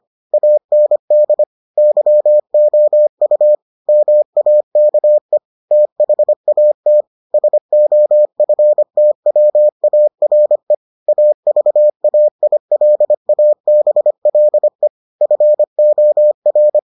Morse code, heard as a keyed tone, about 25 wpm, 600 Hz. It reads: AND YOU MAKE THAT SOFTWARE AVAILABLE FOR